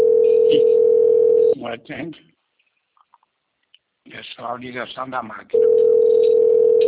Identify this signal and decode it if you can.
Ringback tone (North America), 440 + 480 Hz, with speech.
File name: rg-1600-unknown- - - .30025